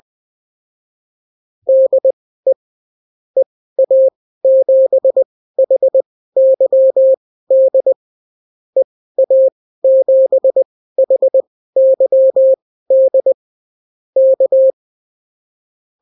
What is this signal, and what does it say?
Morse code, heard as a keyed tone, about 20 wpm, spaced as for 14 wpm, 530 Hz. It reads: DE EA7HYD EA7HYD K
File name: CW QSO libroCW4 20-10ppm